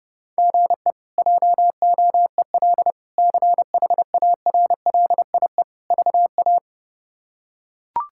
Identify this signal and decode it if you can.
Morse code, heard as a keyed tone, about 30 wpm, 700 Hz. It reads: GE JOEL CHARLIE VA
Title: K1USN SST QSO Practice - 30wpm